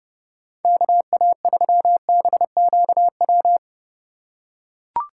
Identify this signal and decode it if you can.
Morse code, heard as a keyed tone, about 30 wpm, 700 Hz. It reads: KA3BQW
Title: USA Call Signs - 6 Characters - 3x Character Spacing - 30wpm